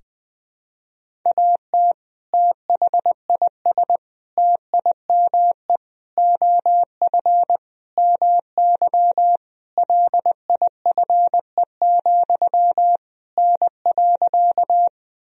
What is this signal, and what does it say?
Morse code, heard as a keyed tone, about 20 wpm, 700 Hz. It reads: AT THIS TIME OF MY LIFE, N.
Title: Sentences from Top 300 Words and Abbreviations 20wpm